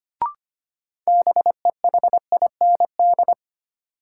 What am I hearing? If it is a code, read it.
Morse code, heard as a keyed tone, about 25 wpm, 700 Hz. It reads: BEHIND
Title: Sentences from Top 500 Words - New Word Drill - 25wpm